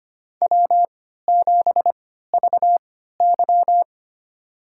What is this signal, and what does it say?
Morse code, heard as a keyed tone, about 25 wpm, spaced as for 14 wpm, 700 Hz. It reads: W7VY
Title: USA Call Signs - 3x Character Spacing - 25wpm